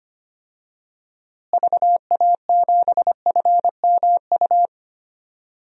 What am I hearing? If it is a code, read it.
Morse code, heard as a keyed tone, about 25 wpm, 700 Hz. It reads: VA7FMU
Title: DXCC-100 Call Signs - 6 Characters - Repeated 2x - 25wpm